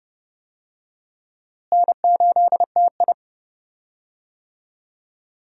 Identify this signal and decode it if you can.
Morse code, heard as a keyed tone, about 30 wpm, 700 Hz. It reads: N8TI
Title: USA Call Signs - 4 Characters - Repeated 2x - 30wpm